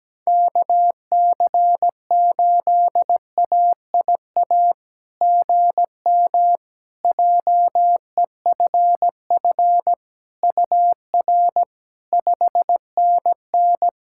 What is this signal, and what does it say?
Morse code, heard as a keyed tone, about 17 wpm, 700 Hz. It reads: KC8AIA GM JEFF UR 5NN
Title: SOTA QSO Practice 17wpm